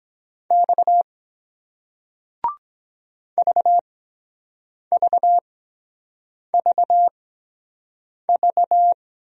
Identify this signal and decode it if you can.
Morse code, heard as a keyed tone, about 23 wpm, spaced as for 7 wpm, 700 Hz. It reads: X VVVV